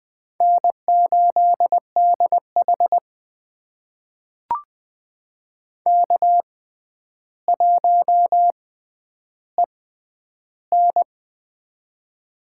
Morse code, heard as a keyed tone, about 21 wpm, spaced as for 10 wpm, 700 Hz. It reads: N8DH K1EN
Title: USA Call Signs - 6x Character Spacing - 20wpm